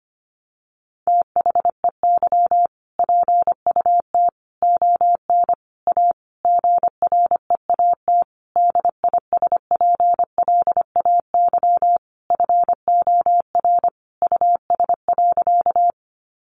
Morse code, heard as a keyed tone, about 25 wpm, 700 Hz. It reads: THEY PUT ON A GREAT DISPLAY FOR US.
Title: Sentences from Top 1200 Words 25wpm